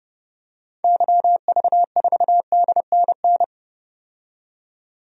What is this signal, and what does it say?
Morse code, heard as a keyed tone, about 30 wpm, 700 Hz. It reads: YV4DNN
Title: DXCC-100 Call Signs - 6 Characters - 30wpm